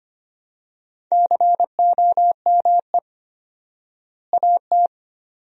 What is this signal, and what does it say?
Morse code, heard as a keyed tone, about 25 wpm, 700 Hz. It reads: COME AT
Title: 4x Word-Spacing - Sentences from Top 300 Words - 25wpm